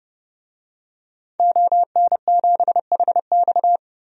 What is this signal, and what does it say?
Morse code, heard as a keyed tone, about 30 wpm, 700 Hz. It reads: ON7HX
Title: DXCC-100 Call Signs - 5 Characters - 30wpm